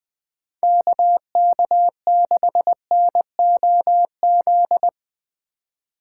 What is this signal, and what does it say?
Morse code, heard as a keyed tone, about 20 wpm, 700 Hz. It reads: KK6NOZ